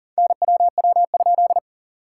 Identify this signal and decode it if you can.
Morse code, heard as a keyed tone, about 40 wpm, 690 Hz. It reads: NWW?
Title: SOTA QSO Practice 40wpm